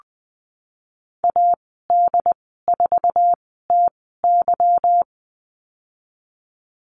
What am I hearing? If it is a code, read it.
Morse code, heard as a keyed tone, about 20 wpm, spaced as for 14 wpm, 700 Hz. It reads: AD4TY
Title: 147 Lesson 24g - Letter y - Call Signs 20wpm